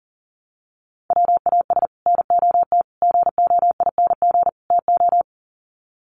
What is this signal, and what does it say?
Morse code, heard as a keyed tone, about 40 wpm, 700 Hz. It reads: WAS NOT GOING TO
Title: Sets of 4 Words 40wpm